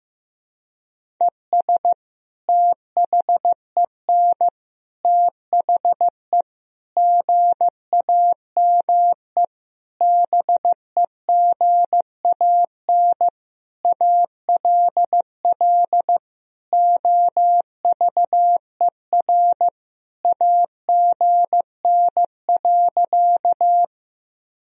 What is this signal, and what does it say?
Morse code, heard as a keyed tone, about 15 wpm, 700 Hz. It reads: ES THEN THE GAME BEGAN ALL OVER AGN.
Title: Sentences from Top 500 Words and Abbreviations 15wpm